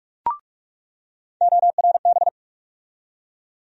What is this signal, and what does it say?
Morse code, heard as a keyed tone, about 44 wpm, 700 Hz. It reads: ORB
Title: USA Call Sign Postfixes - 3 Letters 45wpm